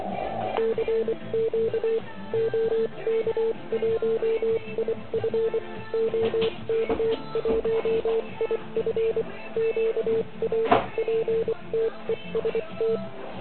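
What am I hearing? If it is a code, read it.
Morse code, heard as a keyed tone, about 24 wpm, spaced as for 21 wpm, 440 Hz, with noise. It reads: CQ OK1IF OK1IF QRPTEST
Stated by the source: Morse code from the OK QRP zavod 2018 contest, Chrudim.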